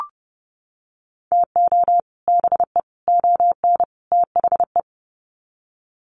Morse code, heard as a keyed tone, about 30 wpm, 700 Hz. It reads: TO BE ON THE
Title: Sets of 4 Words 30wpm